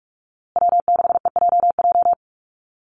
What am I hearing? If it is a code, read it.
Morse code, heard as a keyed tone, about 44 wpm, 700 Hz. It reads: W6EJJ
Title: USA Call Signs Repeated 3x 45wpm